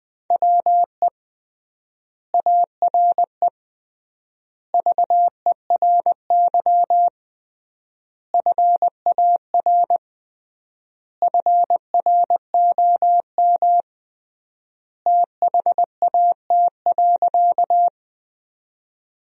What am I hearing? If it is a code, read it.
Morse code, heard as a keyed tone, about 20 wpm, 700 Hz. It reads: WE ARE VERY FAR FROM THAT.
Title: 3x Word-Spacing - Sentences from Top 300 Words - 20wpm